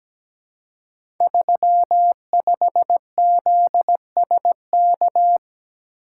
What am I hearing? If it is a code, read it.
Morse code, heard as a keyed tone, about 17 wpm, 700 Hz. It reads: 35ZSK
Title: Code Groups - 5 Letter:Numbers - 17wpm